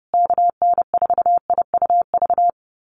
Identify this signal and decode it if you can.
Morse code, heard as a keyed tone, about 30 wpm, 700 Hz. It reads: KN4IUV
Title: USA Call Signs Repeated 3x 30wpm